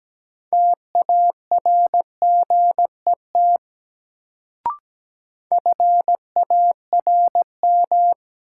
Morse code, heard as a keyed tone, about 17 wpm, 700 Hz. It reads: TARGET FARM
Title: Sentences from Top 1200 Words - New Word Drill Alt - 17wpm